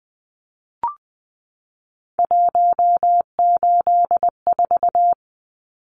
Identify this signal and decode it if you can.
Morse code, heard as a keyed tone, about 20 wpm, 700 Hz. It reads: 184
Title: Numbers - 3-Digit 20wpm